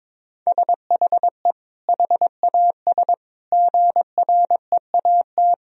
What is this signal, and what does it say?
Morse code, heard as a keyed tone, about 22 wpm, 700 Hz. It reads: SHE HAS GREAT
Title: Sentences from Top 1000 Words - Review Part 4 - 22wpm